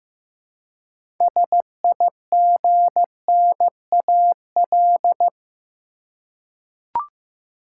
Morse code, heard as a keyed tone, about 15 wpm, 700 Hz. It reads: SIGNAL
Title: Six Letter Words - 15wpm